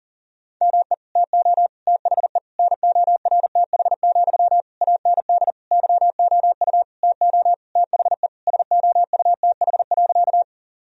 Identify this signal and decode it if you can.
Morse code, heard as a keyed tone, about 40 wpm, 700 Hz. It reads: ME TO THE NORTH, AND YOU TO THE SOUTH.
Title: Sentences from Top 1000 Words - Review Part 2 - 40wpm